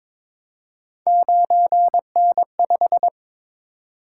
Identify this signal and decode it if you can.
Morse code, heard as a keyed tone, about 22 wpm, 700 Hz. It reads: 9N5
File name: practice / 22wpm / Three Letter-Number Combinations - 22wpm